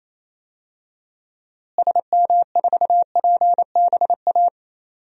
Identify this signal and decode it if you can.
Morse code, heard as a keyed tone, about 28 wpm, 700 Hz. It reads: SM4PBA
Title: DXCC-100 Call Signs - 6 Characters - Repeated 3x - 28wpm